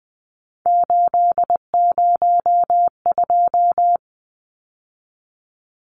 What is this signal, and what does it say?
Morse code, heard as a keyed tone, about 20 wpm, 700 Hz. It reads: 802
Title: Numbers - 3-Digit 20wpm